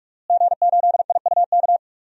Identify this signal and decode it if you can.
Morse code, heard as a keyed tone, about 46 wpm, 700 Hz. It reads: G8IUK
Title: DXCC-100 Call Signs - 5 Characters - Repeated 3x - 45wpm